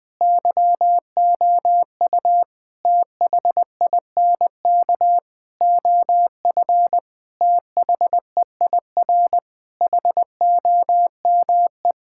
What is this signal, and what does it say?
Morse code, heard as a keyed tone, about 20 wpm, 700 Hz. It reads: YOU THINK OF THEIR HOME
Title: Sentences from Top 1000 Words - Review Part 5 - 20wpm